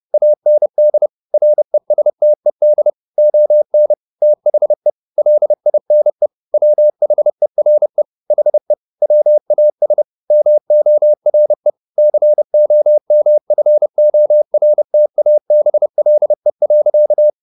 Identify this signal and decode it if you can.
Morse code, heard as a keyed tone, about 30 wpm, 600 Hz. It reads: AND RESTED ON THE LINE WHERE HE WAS MORE COMFORTABLE.
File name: the old man and the sea / the old man and the sea 1 30wpm